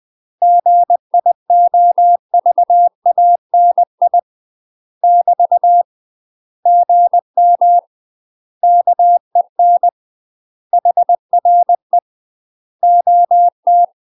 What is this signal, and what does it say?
Morse code, heard as a keyed tone, about 20 wpm, 700 Hz. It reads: GIOVANI <BT> GM KEN HRE OT
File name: UK MASTER QSOs TRAINING / 20WPM / UK MASTER QSOs 0031